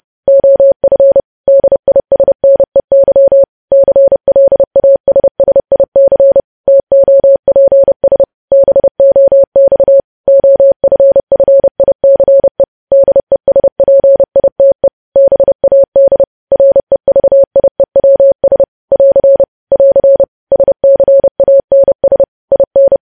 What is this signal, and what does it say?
Morse code, heard as a keyed tone, about 30 wpm, 550 Hz. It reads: OF DISNEY CLASSIC TOPS BOX OFFICE DESPITE BAD REVIEWS <AR> <AR> SCANS IN
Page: news headlines in morse code at 30 wpm